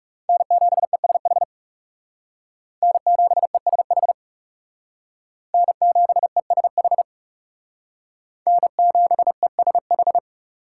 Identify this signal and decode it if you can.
Morse code, heard as a keyed tone, about 38 wpm, 700 Hz. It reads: N7ESH N7ESH N7ESH N7ESH